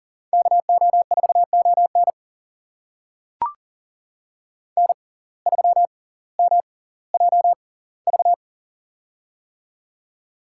Morse code, heard as a keyed tone, about 40 wpm, spaced as for 21 wpm, 690 Hz. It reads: KO4ON N3MJV
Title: USA Call Signs - 5 Characters - 6x Character Spacing - 40wpm